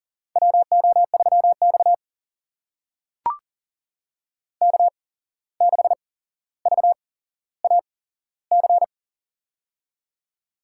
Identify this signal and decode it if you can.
Morse code, heard as a keyed tone, about 38 wpm, spaced as for 16 wpm, 700 Hz. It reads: WO3X K6VAC
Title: USA Call Signs - 8x Character Spacing - 40wpm